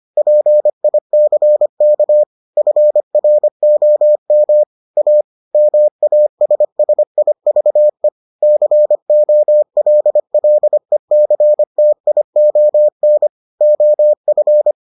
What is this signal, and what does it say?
Morse code, heard as a keyed tone, about 25 wpm, 600 Hz. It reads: PICK FROM A MASSIVE COLLECTION OF